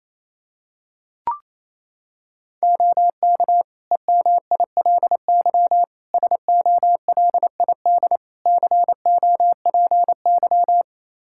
Morse code, heard as a keyed tone, about 28 wpm, 700 Hz. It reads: OK EMILY SOLID COPY
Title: Practice QSO - Exchange 3 - More Info - 28wpm